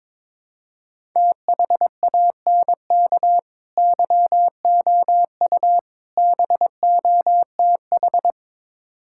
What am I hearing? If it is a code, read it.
Morse code, heard as a keyed tone, about 22 wpm, 700 Hz. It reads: THANK YOU BOTH
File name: Build A Phrase - 5-Word Phrases - 22wpm